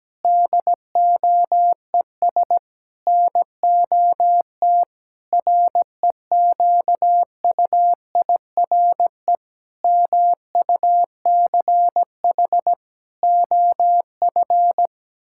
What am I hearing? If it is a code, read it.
Morse code, heard as a keyed tone, about 17 wpm, 700 Hz. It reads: DOES NOT REQUIRE MUCH OF
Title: Sentences from Top 1000 Words - Review Part 6 - 17wpm